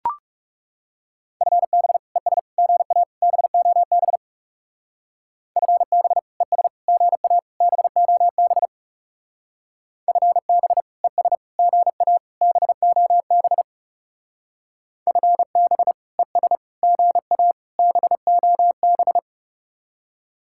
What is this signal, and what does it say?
Morse code, heard as a keyed tone, about 36 wpm, 700 Hz. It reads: FB ES GA BOB FB ES GA BOB FB ES GA BOB FB ES GA BOB
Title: Speed-Racing Common QSO Phrases - 30wpm